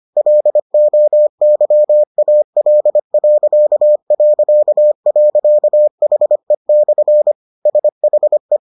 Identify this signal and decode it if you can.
Morse code, heard as a keyed tone, about 25 wpm, 600 Hz. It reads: LOYAL...HE/ SHE